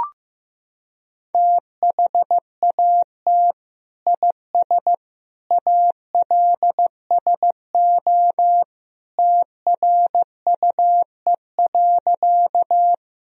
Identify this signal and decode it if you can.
Morse code, heard as a keyed tone, about 15 wpm, 700 Hz. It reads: THAT IS ALSO TRUE.
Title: Sentences from Top 1000 Words - Review Part 1 - 15wpm